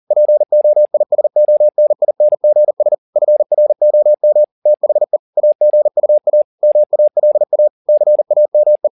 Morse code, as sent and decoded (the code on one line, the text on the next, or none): .--. --- .. ... --- -. .. -. --. ... / ..-. .-. --- -- / - .... / .- --. ..- .- / -- .- .-.. .- / -.-. .- -- .
POISONINGS FROM THE AGUA MALA CAME